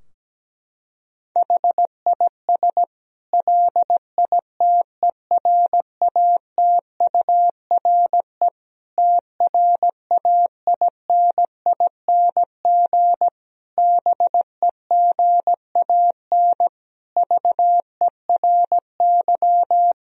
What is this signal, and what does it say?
Morse code, heard as a keyed tone, about 17 wpm, 700 Hz. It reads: HIS LITERATURE TRAINING BEGAN VERY
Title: Sentences from Top 1000 Words - Review Part 4 - 17wpm